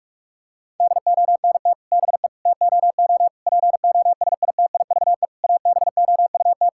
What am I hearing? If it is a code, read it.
Morse code, heard as a keyed tone, about 46 wpm, 700 Hz. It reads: DONT BE TOO POSITIVE ABOUT
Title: Sentences from Top 1000 Words - Review Part 5 - 45wpm